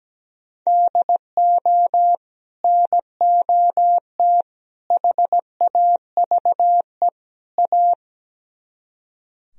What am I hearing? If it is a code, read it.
Morse code, heard as a keyed tone, about 17 wpm, 700 Hz. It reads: DO NOT HAVE A
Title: Speed-Racing - Sets of 4 Words 17wpm